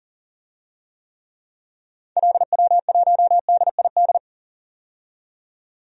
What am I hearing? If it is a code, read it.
Morse code, heard as a keyed tone, about 40 wpm, 700 Hz. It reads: LW1DID
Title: DXCC-100 Call Signs - 6 Characters - 40wpm